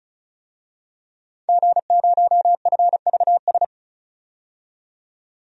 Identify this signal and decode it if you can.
Morse code, heard as a keyed tone, about 36 wpm, 700 Hz. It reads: G0FVS
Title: DXCC-100 Call Signs - 5 Characters - 35wpm